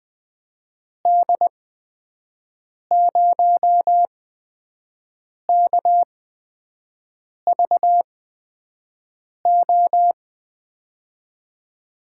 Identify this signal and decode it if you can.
Morse code, heard as a keyed tone, about 21 wpm, spaced as for 6 wpm, 700 Hz. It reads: D0KVO
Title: USA Call Signs - 6 Characters - 8x Character Spacing - 20wpm